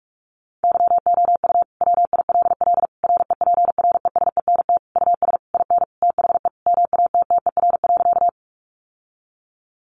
Morse code, heard as a keyed tone, about 44 wpm, 700 Hz. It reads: YOU WILL REPRESENT US IN THE MATTER.
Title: Sentences from Top 1200 Words 45wpm